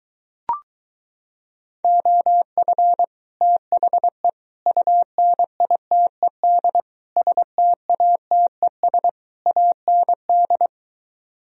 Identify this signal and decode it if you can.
Morse code, heard as a keyed tone, about 23 wpm, 700 Hz. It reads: OF THE UNITED STATES AND